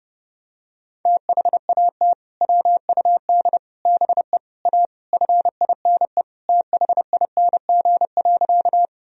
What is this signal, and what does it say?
Morse code, heard as a keyed tone, about 30 wpm, 700 Hz. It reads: THAT WUD BE A FINE THING.
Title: Sentences from Top 500 Words and Abbreviations 30wpm